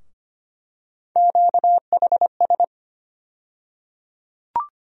Morse code, heard as a keyed tone, about 25 wpm, 700 Hz. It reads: QHS
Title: USA Call Sign Postfixes - 3 Letters 25wpm